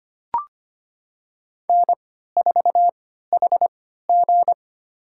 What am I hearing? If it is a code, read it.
Morse code, heard as a keyed tone, about 25 wpm, spaced as for 14 wpm, 700 Hz. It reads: N4HG
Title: USA Call Signs - 4 Characters - 3x Character Spacing - 25wpm